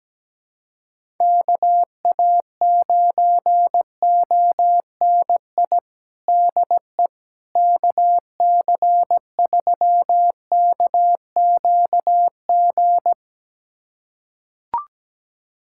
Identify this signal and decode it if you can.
Morse code, heard as a keyed tone, about 17 wpm, 700 Hz. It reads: KA9ONI DE KC3KQG
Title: USA Callsign DE Callsign - 6 Characters - 17wpm